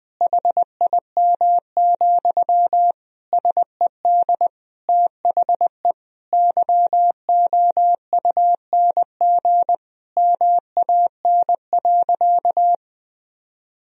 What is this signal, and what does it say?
Morse code, heard as a keyed tone, about 21 wpm, 700 Hz. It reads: HIM, SED THE YOUNG MAN.
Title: Sentences from Top 400 Words and Abbreviations 20wpm